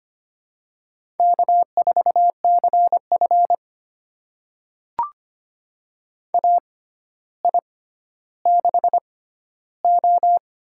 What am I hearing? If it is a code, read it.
Morse code, heard as a keyed tone, about 25 wpm, spaced as for 13 wpm, 700 Hz. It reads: K4CF AI6O
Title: USA Call Signs - 6x Character Spacing - 25wpm